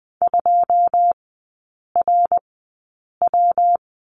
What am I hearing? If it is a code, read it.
Morse code, heard as a keyed tone, about 20 wpm, spaced as for 8 wpm, 700 Hz. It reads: 2RW
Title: 133 Lesson 22e - Question Mark - Three Characters 20wpm